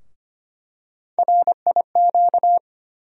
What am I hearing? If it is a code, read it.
Morse code, heard as a keyed tone, about 25 wpm, 700 Hz. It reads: RIQ
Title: USA Call Sign Postfixes - 3 Letters 25wpm